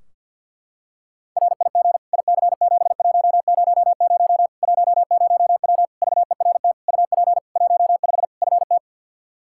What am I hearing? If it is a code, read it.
Morse code, heard as a keyed tone, about 52 wpm, 700 Hz. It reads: RIG IC7100 10W VERT UP 15 FT
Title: Practice QSO - Exchange 2 - Info - 50wpm